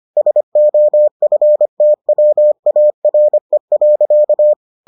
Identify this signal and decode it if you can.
Morse code, heard as a keyed tone, about 25 wpm, 600 Hz. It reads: SOFTWARE.